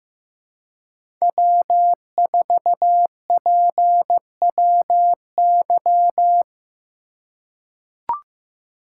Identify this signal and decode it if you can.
Morse code, heard as a keyed tone, about 15 wpm, 700 Hz. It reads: W4PWY
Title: USA Call Signs - 5 Characters - 4x Character Spacing - 15wpm